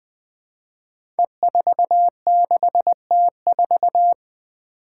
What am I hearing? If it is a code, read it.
Morse code, heard as a keyed tone, about 21 wpm, 700 Hz. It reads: E46T4
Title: Five Challenging Letter-Number Combinations 20wpm